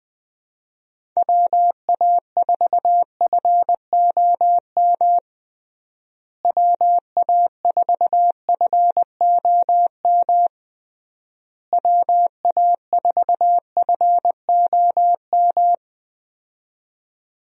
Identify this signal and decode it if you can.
Morse code, heard as a keyed tone, about 20 wpm, 700 Hz. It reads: WA4FOM WA4FOM WA4FOM
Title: USA Call Signs - 6 Characters - Repeated 3x - 20wpm